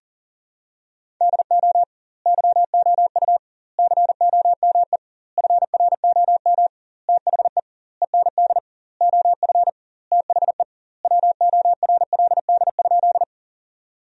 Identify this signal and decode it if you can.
Morse code, heard as a keyed tone, about 40 wpm, 700 Hz. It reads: DO YOU COME FROM THE END OF THE WORLD?
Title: 2x Word-Spacing - Sentences from Top 200 Words - 40wpm